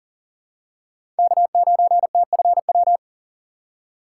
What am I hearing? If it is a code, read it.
Morse code, heard as a keyed tone, about 37 wpm, 700 Hz. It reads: K9TFW
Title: USA Call Signs - 5 Characters - 6x Character Spacing - 40wpm